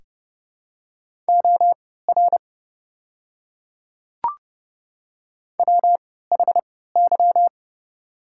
Morse code, heard as a keyed tone, about 30 wpm, spaced as for 17 wpm, 700 Hz. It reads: OR WHY
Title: 3x Character Spacing - Top 300 Words - 30wpm